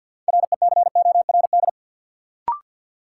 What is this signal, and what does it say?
Morse code, heard as a keyed tone, about 50 wpm, 700 Hz. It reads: RECORD
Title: Sentences from Top 500 Words - New Word Drill - 50wpm